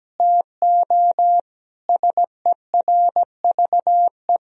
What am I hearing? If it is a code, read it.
Morse code, heard as a keyed tone, about 17 wpm, 700 Hz. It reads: TO SERVE
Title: Sentences from Top 1000 Words - Review Part 2 - 17wpm